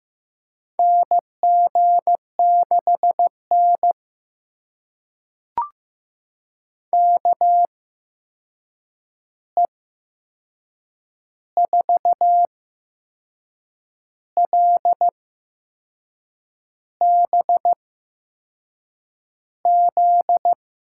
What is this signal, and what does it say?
Morse code, heard as a keyed tone, about 15 wpm, spaced as for 6 wpm, 700 Hz. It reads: NG6N KE4LBZ
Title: USA Call Signs - 8x Character Spacing - 15wpm